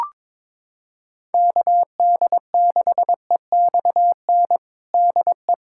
Morse code, heard as a keyed tone, about 22 wpm, 700 Hz. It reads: KD6EXN DE
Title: USA Callsign DE Callsign 22wpm